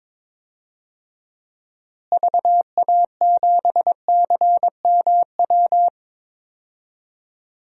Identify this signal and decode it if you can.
Morse code, heard as a keyed tone, about 21 wpm, 700 Hz. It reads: VA7CMW
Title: DXCC-100 Call Signs - 6 Characters - Repeated 3x - 22wpm